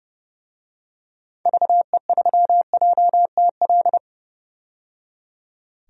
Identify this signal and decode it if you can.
Morse code, heard as a keyed tone, about 30 wpm, 700 Hz. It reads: VE3JTL
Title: DXCC-100 Call Signs - 6 Characters - 30wpm